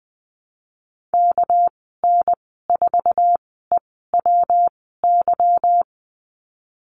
Morse code, heard as a keyed tone, about 20 wpm, spaced as for 15 wpm, 700 Hz. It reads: KN4EWY
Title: 213 Lesson 34g - Number 8 - Call Signs 20wpm